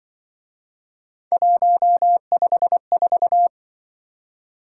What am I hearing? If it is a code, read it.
Morse code, heard as a keyed tone, about 24 wpm, 700 Hz. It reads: 154
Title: Speed-Racing - Three Letter-Number Combinations - 20wpm